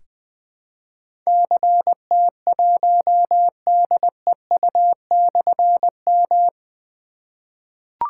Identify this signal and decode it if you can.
Morse code, heard as a keyed tone, about 20 wpm, 700 Hz. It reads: CT1DEU/M
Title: Difficult Call Signs 20wpm